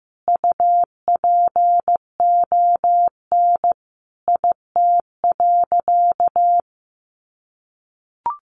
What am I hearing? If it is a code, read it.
Morse code, heard as a keyed tone, about 15 wpm, 700 Hz. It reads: UPON IT.
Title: Sentences from Top 1500 Words 15wpm